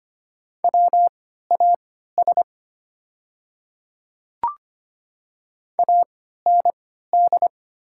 Morse code, heard as a keyed tone, about 25 wpm, spaced as for 14 wpm, 700 Hz. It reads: WAS AND
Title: 3x Character Spacing - Top 100 Words - 25wpm